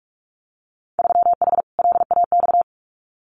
Morse code, heard as a keyed tone, about 46 wpm, 700 Hz. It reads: 3H LAX
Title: Winter Field Day Exchanges 45wpm